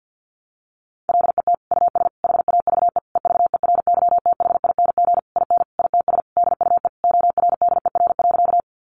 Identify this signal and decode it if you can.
Morse code, heard as a keyed tone, about 50 wpm, 700 Hz. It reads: LET US HAVE EVERYTHING IN ITS DUE ORDER.
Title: Sentences from Top 900 Words 50wpm